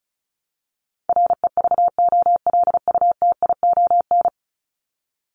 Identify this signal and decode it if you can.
Morse code, heard as a keyed tone, about 34 wpm, 700 Hz. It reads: REVOLUTION
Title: Top 1751-2000 Words 35wpm